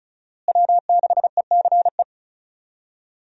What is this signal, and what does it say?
Morse code, heard as a keyed tone, about 36 wpm, 700 Hz. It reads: W6ECE
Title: USA Call Signs - 5 Characters - Repeated 2x - 35wpm